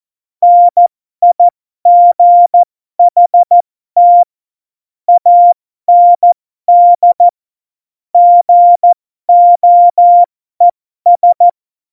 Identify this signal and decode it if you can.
Morse code, heard as a keyed tone, about 14 wpm, spaced as for 12 wpm, 700 Hz. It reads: NIGHT AND GOES
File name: Morsekurs / Witze